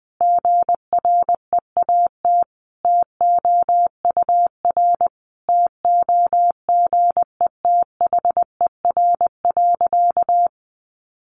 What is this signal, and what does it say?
Morse code, heard as a keyed tone, about 20 wpm, 700 Hz. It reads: GREAT TOUR TOGETHER.
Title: Sentences from Top 1400 Words 20wpm